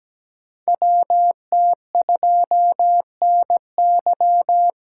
Morse code, heard as a keyed tone, about 17 wpm, 700 Hz. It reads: WT2NY
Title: USA Call Signs - 5 Characters - 3x Character Spacing - 17wpm